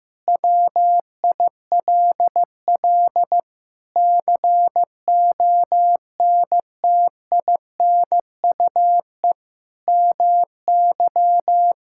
Morse code, heard as a keyed tone, about 16 wpm, 700 Hz. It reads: WILL CONTINUE MY
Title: Sentences from Top 1000 Words - Review Part 6 - 15wpm